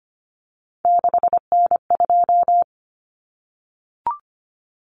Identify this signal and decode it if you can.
Morse code, heard as a keyed tone, about 25 wpm, 700 Hz. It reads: 6N2
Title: Common Cut Numbers - 3-Digit 25wpm